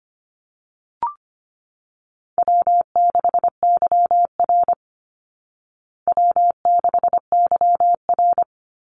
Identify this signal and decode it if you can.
Morse code, heard as a keyed tone, about 25 wpm, 700 Hz. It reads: W6YR W6YR
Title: Easy Call Signs Repeated 2x - 25wpm